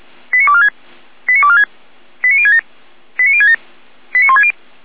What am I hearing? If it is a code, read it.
Five-tone selective call (ZVEI), heard as quick runs of five tones, tones of about 70 ms. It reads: 89246 89246 89067 89067 89160